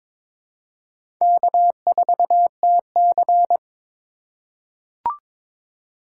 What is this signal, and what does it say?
Morse code, heard as a keyed tone, about 22 wpm, 700 Hz. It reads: K4TC
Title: USA Call Signs - 4 Characters - 8x Character Spacing - 22wpm